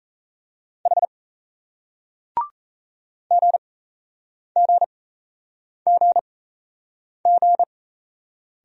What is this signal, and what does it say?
Morse code, heard as a keyed tone, about 38 wpm, spaced as for 8 wpm, 700 Hz. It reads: H GGGG